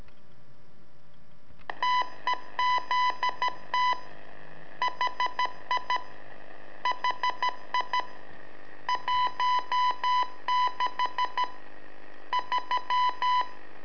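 Morse code, heard as a keyed tone, about 13 wpm, 1000 Hz. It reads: TEZT HI HI 16 3
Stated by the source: Telemetry in Morse from LU7AA and LU4EZ, AMSAT and Turma Mar del Plata, sent from a balloon.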